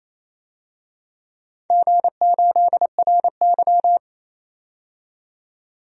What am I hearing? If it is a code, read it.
Morse code, heard as a keyed tone, about 27 wpm, 700 Hz. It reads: G8RY
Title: DXCC-100 Call Signs - 4 Characters - 28wpm